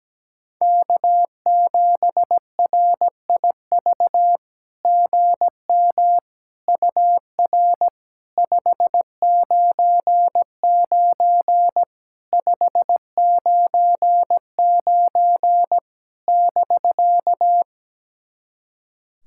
Morse code, heard as a keyed tone, about 17 wpm, 700 Hz. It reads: K7RIV GM UR 599 599 <BK>
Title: SOTA QSO Practice 17wpm